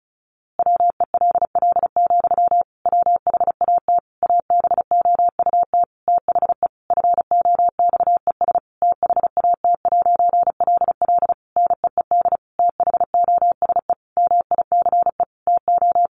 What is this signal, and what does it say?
Morse code, heard as a keyed tone, about 34 wpm, 700 Hz. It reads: WELL, WHAT ABOUT THE FOXES THAT'LL NEED THOSE MICE TO
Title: A Sound Of Thunder 35wpm